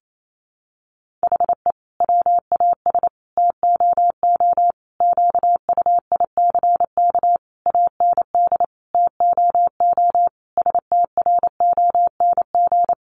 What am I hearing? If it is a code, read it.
Morse code, heard as a keyed tone, about 28 wpm, 700 Hz. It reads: HE WAS TOO QUICK AND TOO STRONG
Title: Sentences from Top 600 Words 28wpm